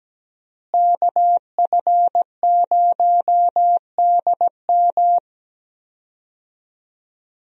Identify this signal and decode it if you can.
Morse code, heard as a keyed tone, about 17 wpm, 700 Hz. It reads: KF0DM
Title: USA Call Signs - 5 Characters - Repeated 2x - 17wpm